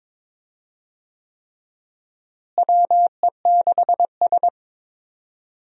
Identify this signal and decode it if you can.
Morse code, heard as a keyed tone, about 22 wpm, 700 Hz. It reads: WE6S
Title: USA Call Signs - 4 Characters - Repeated 3x - 22wpm